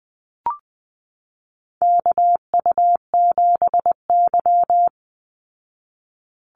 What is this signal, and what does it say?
Morse code, heard as a keyed tone, about 20 wpm, 700 Hz. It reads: KU7Y
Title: Call Signs Repeated 2x - 20wpm